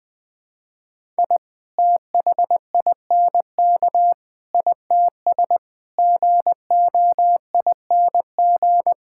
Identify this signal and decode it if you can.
Morse code, heard as a keyed tone, about 21 wpm, 700 Hz. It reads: I THINK ITS GOING
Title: Sets of 6 Words - 20wpm